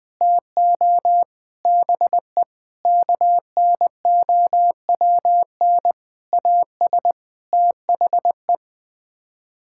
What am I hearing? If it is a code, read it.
Morse code, heard as a keyed tone, about 20 wpm, 700 Hz. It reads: TO BE KNOWN AS THE
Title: Sets of 8 Words - 20wpm